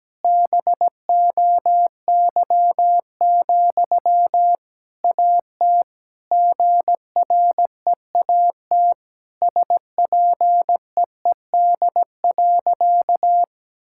Morse code, heard as a keyed tone, about 17 wpm, 700 Hz. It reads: BOY, AT GREAT SPEED.
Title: Sentences from Top 1000 Words - Review Part 5 - 17wpm